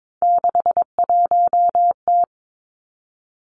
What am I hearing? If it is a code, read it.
Morse code, heard as a keyed tone, about 22 wpm, 700 Hz. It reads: 61T
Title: Three Challenging Letter-Number Combinations 22wpm